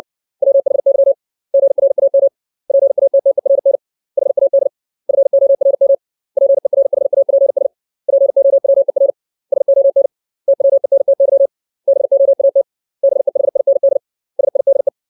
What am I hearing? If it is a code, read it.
Morse code, heard as a keyed tone, about 60 wpm, 550 Hz. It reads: WHY GRAM PATTERN HAD FORM PERHAPS POWER SON TWENTY BOAT BEHIND SIDE
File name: 632 words / 632 Words 60WPM